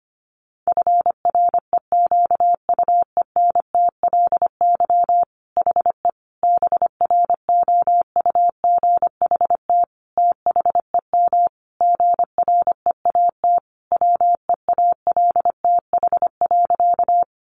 Morse code, heard as a keyed tone, about 25 wpm, 700 Hz. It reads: FREQUENTLY HE BROUGHT THEM GREAT WEALTH.
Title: Sentences from Top 1300 Words 25wpm